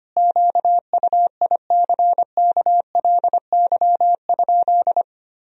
Morse code, heard as a keyed tone, about 25 wpm, 700 Hz. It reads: QUICKLY?
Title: Sentences from Top 1000 Words - Review Part 3 - 25wpm